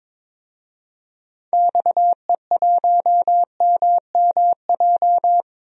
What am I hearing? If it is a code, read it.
Morse code, heard as a keyed tone, about 22 wpm, 700 Hz. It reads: XE1MMJ
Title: DXCC-100 Call Signs - 6 Characters - Repeated 2x - 22wpm